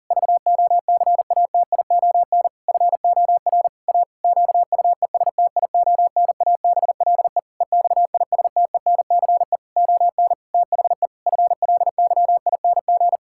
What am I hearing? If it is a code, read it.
Morse code, heard as a keyed tone, about 40 wpm, 700 Hz. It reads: VOCATION FOR A QUESTIONABLE EXISTENCE ON THE FLYING
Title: Pangram Challenge v2 - 40wpm